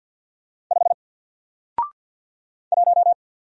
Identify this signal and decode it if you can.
Morse code, heard as a keyed tone, about 50 wpm, 700 Hz. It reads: 5 1
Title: Single Letter-Number 50wpm